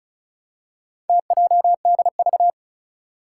Morse code, heard as a keyed tone, about 36 wpm, 700 Hz. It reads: TJDV
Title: Four Challenging Letter-Number Combinations 35wpm